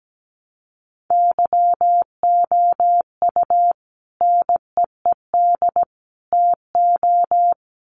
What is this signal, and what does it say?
Morse code, heard as a keyed tone, about 17 wpm, 700 Hz. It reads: YOU NEED TO
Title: Sets of 3 Words 17wpm